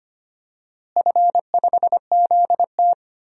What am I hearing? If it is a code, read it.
Morse code, heard as a keyed tone, about 25 wpm, 700 Hz. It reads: F5ZT
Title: DXCC-100 Call Signs - 4 Characters - Repeated 3x - 25wpm